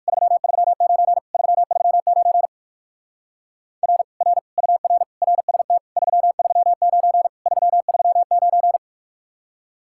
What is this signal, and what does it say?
Morse code, heard as a keyed tone, about 58 wpm, 700 Hz. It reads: 339 339 R R UR RST 339 339